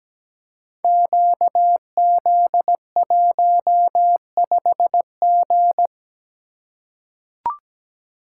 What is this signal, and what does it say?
Morse code, heard as a keyed tone, about 17 wpm, 700 Hz. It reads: QZ15G
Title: Code Groups - 5 Letter:Numbers - 17wpm